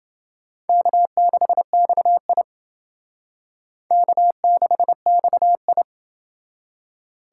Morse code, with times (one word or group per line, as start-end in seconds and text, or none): 0.7-2.4 K6XI
3.9-5.8 K6XI